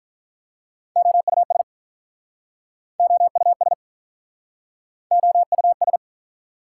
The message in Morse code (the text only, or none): OUS OUS OUS